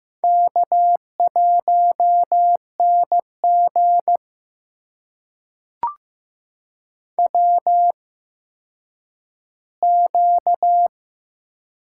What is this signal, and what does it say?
Morse code, heard as a keyed tone, about 15 wpm, spaced as for 9 wpm, 700 Hz. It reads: K1NG WQ
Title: USA Call Signs - 4 Characters - 8x Character Spacing - 15wpm